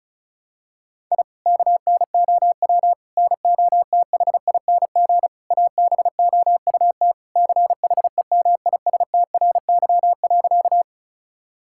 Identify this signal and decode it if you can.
Morse code, heard as a keyed tone, about 36 wpm, 700 Hz. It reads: I KNOW NOTHING ABOUT CHEMISTRY.
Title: Sentences from Top 1000 Words - Review Part 6 - 35wpm